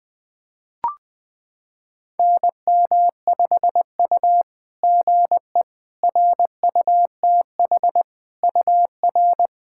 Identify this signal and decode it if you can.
Morse code, heard as a keyed tone, about 20 wpm, 690 Hz. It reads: NM5U GE RUTH UR
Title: SOTA QSO Practice 20wpm